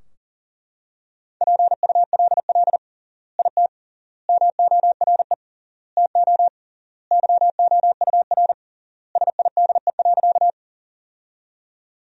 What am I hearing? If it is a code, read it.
Morse code, heard as a keyed tone, about 40 wpm, 700 Hz. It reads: PULL IT MORE TO YOUR SIDE.
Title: 3x Word-Spacing - Sentences from Top 400 Words - 40wpm